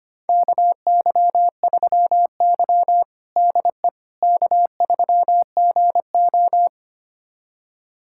Morse code, heard as a keyed tone, about 25 wpm, 700 Hz. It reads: KY3Y DE K3GO
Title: USA Callsign DE Callsign - 4 Characters - 25wpm